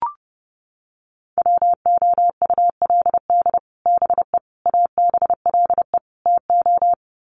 Morse code, heard as a keyed tone, about 30 wpm, 700 Hz. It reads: WOULD BE ABLE TO
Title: Sets of 4 Words 30wpm